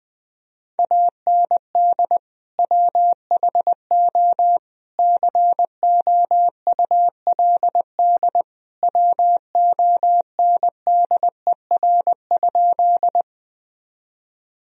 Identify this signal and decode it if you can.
Morse code, heard as a keyed tone, about 21 wpm, 700 Hz. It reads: AND WHO COULD WONDER?
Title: Sentences from Top 1000 Words - Review Part 3 - 20wpm